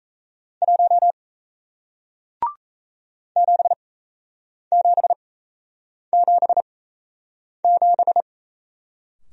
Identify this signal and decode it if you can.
Morse code, heard as a keyed tone, about 38 wpm, spaced as for 8 wpm, 700 Hz. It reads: 1 7777